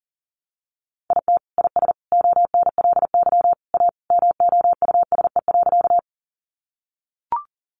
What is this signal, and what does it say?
Morse code, heard as a keyed tone, about 40 wpm, 700 Hz. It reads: IT IS ONLY A MOUSE.
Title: Sentences from Top 700 Words 40wpm